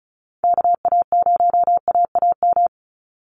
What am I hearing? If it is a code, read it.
Morse code, heard as a keyed tone, about 34 wpm, 700 Hz. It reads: KA0AAM
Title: Call Signs Repeated 2x - 35wpm